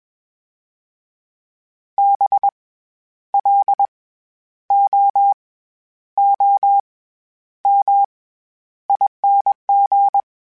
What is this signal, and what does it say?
Morse code, heard as a keyed tone, about 21 wpm, spaced as for 10 wpm, 800 Hz. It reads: BLOOMING
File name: Special INGs-mp3s / Special INGs-mp3s / SlowToFaster ing @20